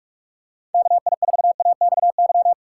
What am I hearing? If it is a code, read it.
Morse code, heard as a keyed tone, about 47 wpm, 700 Hz. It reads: KI4AXY
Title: USA Call Signs - 2x Character Spacing - 45wpm